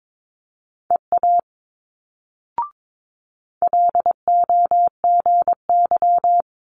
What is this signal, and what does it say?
Morse code, heard as a keyed tone, about 22 wpm, 700 Hz. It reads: EA LOGY
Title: Multi-letter Phonemes Quick Fire 22wpm